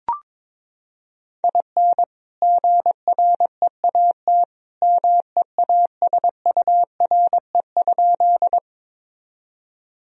Morse code, heard as a keyed tone, about 22 wpm, 700 Hz. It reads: IN GREAT MEASURE?
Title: Sentences from Top 1000 Words - Review Part 2 - 22wpm